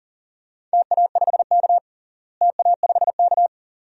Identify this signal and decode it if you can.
Morse code, heard as a keyed tone, about 40 wpm, 700 Hz. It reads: TA5K TA5K
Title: DXCC-100 Call Signs - 4 Characters - Repeated 2x - 40wpm